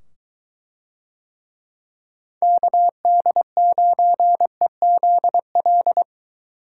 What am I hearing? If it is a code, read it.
Morse code, heard as a keyed tone, about 23 wpm, 700 Hz. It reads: KD9EZL